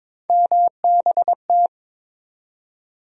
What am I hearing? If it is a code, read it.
Morse code, heard as a keyed tone, about 22 wpm, 700 Hz. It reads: MBT
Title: USA Call Sign Postfixes - 3 Letters 22wpm